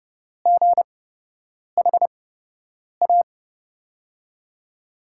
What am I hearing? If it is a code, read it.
Morse code, heard as a keyed tone, about 32 wpm, spaced as for 8 wpm, 700 Hz. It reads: GHA
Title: USA Call Signs - 5 Characters - 8x Character Spacing - 30wpm